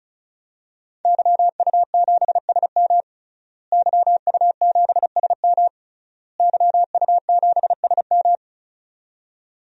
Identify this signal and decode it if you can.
Morse code, heard as a keyed tone, about 35 wpm, 700 Hz. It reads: YU7SM YU7SM YU7SM
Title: DXCC-100 Call Signs - 5 Characters - Repeated 3x - 35wpm